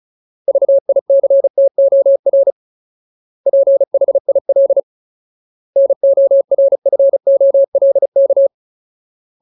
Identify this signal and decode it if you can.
Morse code, heard as a keyed tone, about 34 wpm, 550 Hz. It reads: VICTOR PHIL NORFOLK
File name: Ultimate Word List / 35WPM Ultimate